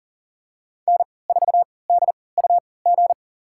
Morse code, heard as a keyed tone, about 40 wpm, spaced as for 23 wpm, 700 Hz. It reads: N4DUG
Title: USA Call Signs - 5 Characters - 3x Character Spacing - 40wpm